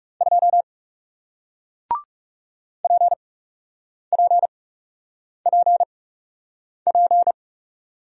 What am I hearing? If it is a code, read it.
Morse code, heard as a keyed tone, about 40 wpm, spaced as for 8 wpm, 700 Hz. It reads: 2 PPPP